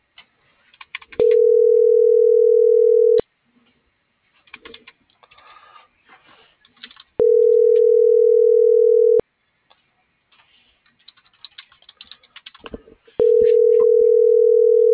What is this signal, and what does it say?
Ringback tone (North America), 440 + 480 Hz.